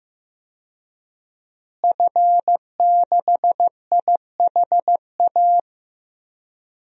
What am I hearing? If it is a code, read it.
Morse code, heard as a keyed tone, about 15 wpm, 700 Hz. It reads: F6IHA
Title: DXCC-100 Call Signs - 5 Characters - Repeated 3x - 15wpm